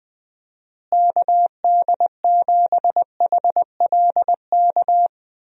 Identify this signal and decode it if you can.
Morse code, heard as a keyed tone, about 20 wpm, 700 Hz. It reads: KD7HLK
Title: USA Call Signs - 6 Characters - Repeated 2x - 20wpm